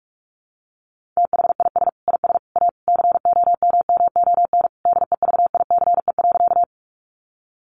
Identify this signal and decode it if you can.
Morse code, heard as a keyed tone, about 44 wpm, 700 Hz. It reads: THIS IS A COMMON DEVICE.
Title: Sentences from Top 1500 Words 45wpm